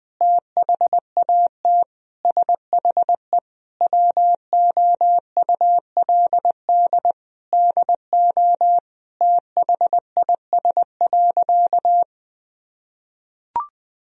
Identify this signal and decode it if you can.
Morse code, heard as a keyed tone, about 19 wpm, 700 Hz. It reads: THAT SHE WOULD DO THIS.
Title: Sentences from Top 1000 Words - Review Part 3 - 20wpm